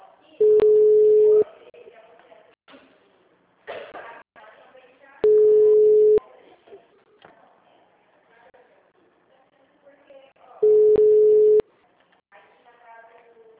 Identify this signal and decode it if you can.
Ringback tone (Europe, 425 Hz), with speech.
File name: out- -2002- - - .61002